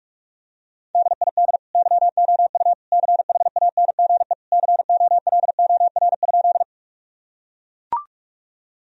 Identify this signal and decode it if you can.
Morse code, heard as a keyed tone, about 41 wpm, 700 Hz. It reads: DID YOU CHANGE COLOR?
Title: Sentences from Top 1000 Words - Review Part 2 - 45wpm